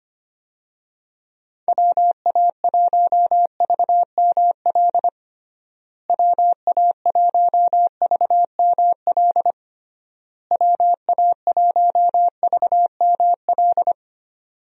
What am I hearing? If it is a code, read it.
Morse code, heard as a keyed tone, about 25 wpm, 700 Hz. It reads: WA1VML WA1VML WA1VML
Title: USA Call Signs - 6 Characters - Repeated 3x - 25wpm